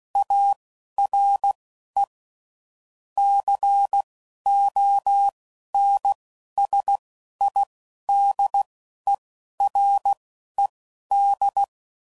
Morse code, heard as a keyed tone, about 16 wpm, spaced as for 11 wpm, 800 Hz. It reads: ARE CONSIDERED
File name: CW Code Groups @10WPM / Technical Subjects / XC